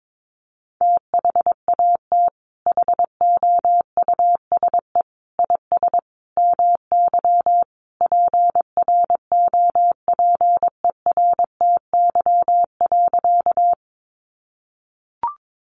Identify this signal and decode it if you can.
Morse code, heard as a keyed tone, about 22 wpm, 700 Hz. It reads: THAT HOUSE IS MY PROPERTY.
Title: Sentences from Top 1000 Words 22wpm